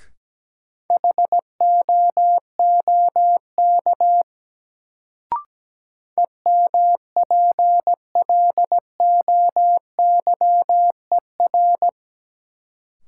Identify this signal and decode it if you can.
Morse code, heard as a keyed tone, about 17 wpm, 700 Hz. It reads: HOOK EMPLOYER
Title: Sentences from Top 1600 Words - New Word Drill - 17wpm